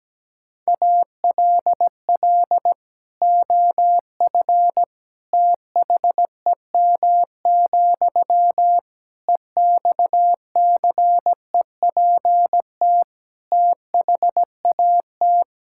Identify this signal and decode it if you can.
Morse code, heard as a keyed tone, about 17 wpm, 700 Hz. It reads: ALL OF THEM, EXCEPT THAT
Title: Sentences from Top 1600 Words 17wpm